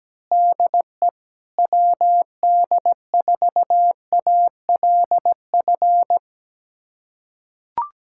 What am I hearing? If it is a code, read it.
Morse code, heard as a keyed tone, about 17 wpm, 700 Hz. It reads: DE WD4ALF
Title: USA Callsign DE Callsign - 6 Characters - 17wpm